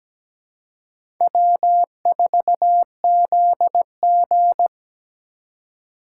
Morse code, heard as a keyed tone, about 17 wpm, 700 Hz. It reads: W4ZG
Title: USA Call Signs - 4 Characters - 3x Character Spacing - 17wpm